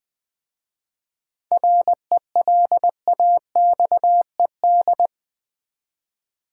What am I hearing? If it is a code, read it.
Morse code, heard as a keyed tone, about 20 wpm, 700 Hz. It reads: RELAXED
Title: 7 Letter Words - First 3 Letters Given - 20wpm